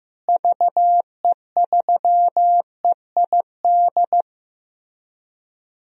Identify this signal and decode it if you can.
Morse code, heard as a keyed tone, about 15 wpm, 700 Hz. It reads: VE3EID
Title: DXCC-100 Call Signs - 6 Characters - Repeated 3x - 15wpm